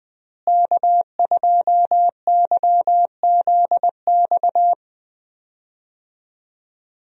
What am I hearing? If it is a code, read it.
Morse code, heard as a keyed tone, about 20 wpm, 700 Hz. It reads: K2YZX
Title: USA Call Signs - 5 Characters - Repeated 3x - 20wpm